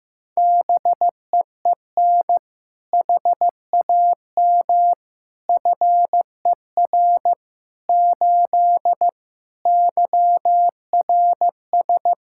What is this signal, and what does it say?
Morse code, heard as a keyed tone, about 15 wpm, 700 Hz. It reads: BEEN HAM FER 8 YRS
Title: Practice QSO - Exchange 3 - More Info - 15wpm